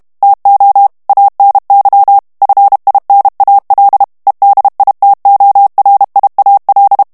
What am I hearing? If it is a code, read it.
Morse code, heard as a keyed tone, about 32 wpm, 800 Hz. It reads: TO ANY FINAL EDITORIAL
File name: morse mp3 / ANGLAIS texte clair 3 vitesse 30 mots minute